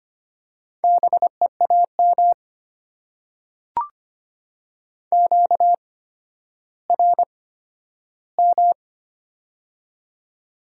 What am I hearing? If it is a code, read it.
Morse code, heard as a keyed tone, about 25 wpm, spaced as for 12 wpm, 700 Hz. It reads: BEAM QRM
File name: Top 90 QSO Elements - 8x Character Spacing - 25wpm